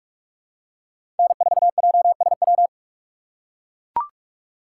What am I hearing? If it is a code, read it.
Morse code, heard as a keyed tone, about 44 wpm, 690 Hz. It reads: N4JSW
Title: USA Call Signs - 5 Characters - 8x Character Spacing - 45wpm